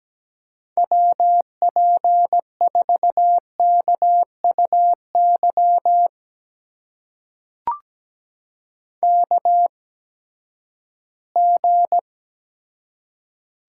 Morse code, heard as a keyed tone, about 17 wpm, spaced as for 12 wpm, 700 Hz. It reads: WP4KUY KG
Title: USA Call Signs - 6 Characters - 8x Character Spacing - 17wpm